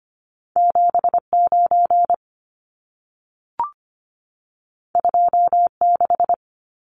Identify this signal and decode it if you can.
Morse code, heard as a keyed tone, about 25 wpm, 700 Hz. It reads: 79 26
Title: Numbers - 2-Digit 25wpm